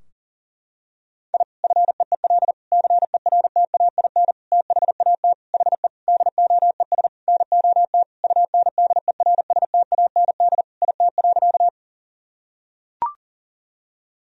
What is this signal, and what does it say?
Morse code, heard as a keyed tone, about 40 wpm, 700 Hz. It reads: I FEEL CERTAIN THAT HE DOES NOT UNDERSTAND IT.
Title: Sentences from Top 1000 Words - Review Part 2 - 40wpm